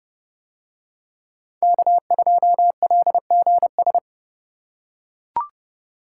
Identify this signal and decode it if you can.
Morse code, heard as a keyed tone, about 29 wpm, 700 Hz. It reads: K2LGS